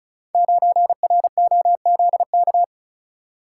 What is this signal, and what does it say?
Morse code, heard as a keyed tone, about 36 wpm, 700 Hz. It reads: 9ROZK
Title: Code Groups - 5 Letter:Numbers - 35wpm